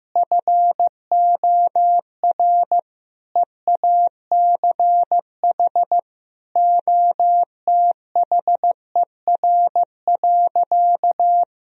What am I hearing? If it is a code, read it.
Morse code, heard as a keyed tone, about 15 wpm, 700 Hz. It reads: FOR EACH OTHER.
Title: Sentences from Top 1000 Words - Review Part 3 - 15wpm